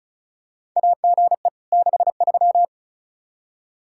Morse code, heard as a keyed tone, about 36 wpm, 700 Hz. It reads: AGE 63
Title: K1USN SST QSO Practice 35wpm